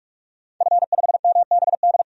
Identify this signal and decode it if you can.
Morse code, heard as a keyed tone, about 44 wpm, 690 Hz. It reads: F5MBD
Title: DXCC-100 Call Signs - 5 Characters - 45wpm